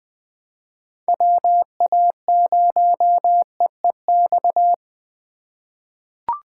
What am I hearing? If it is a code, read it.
Morse code, heard as a keyed tone, about 20 wpm, 700 Hz. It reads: WA0EEX
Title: USA Call Signs - 6 Characters - 2x Character Spacing - 20wpm